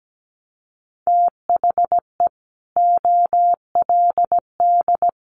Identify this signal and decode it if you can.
Morse code, heard as a keyed tone, about 17 wpm, 700 Hz. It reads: THE OLD
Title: Sentences from Top 1100 Words 17wpm